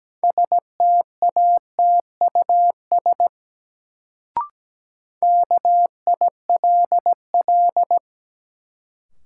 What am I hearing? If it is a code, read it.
Morse code, heard as a keyed tone, about 17 wpm, 700 Hz. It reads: STATUS KILL
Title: Sentences from Top 1400 Words - New Word Drill Alt - 17wpm